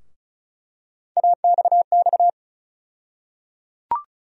Morse code, heard as a keyed tone, about 36 wpm, 700 Hz. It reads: AXX
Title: USA Call Sign Postfixes - 3 Letters 35wpm